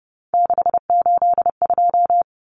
Morse code, heard as a keyed tone, about 30 wpm, 700 Hz. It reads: 682
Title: Common Cut Numbers - 3-Digit 30wpm